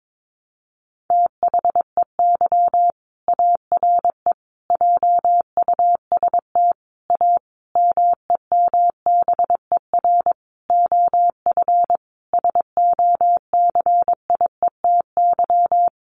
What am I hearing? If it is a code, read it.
Morse code, heard as a keyed tone, about 22 wpm, 700 Hz. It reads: THEY ARE JUST A MEMBER OF SOCIETY
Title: Sentences from Top 1100 Words 22wpm